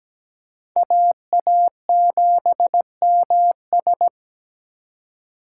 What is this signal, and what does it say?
Morse code, heard as a keyed tone, about 20 wpm, 700 Hz. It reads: AA7MS
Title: Speed Racing - USA 2x2 Call Signs 17wpm